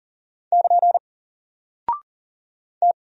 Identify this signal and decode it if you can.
Morse code, heard as a keyed tone, about 38 wpm, 700 Hz. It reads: <KN> T
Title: Single Character (Letter:Number:Punctuation:Prosign) 40wpm